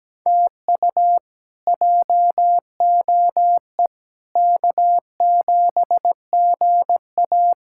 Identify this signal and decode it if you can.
Morse code, heard as a keyed tone, about 17 wpm, 700 Hz. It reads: TU JOE K7GA
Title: K1USN SST QSO Practice - 17wpm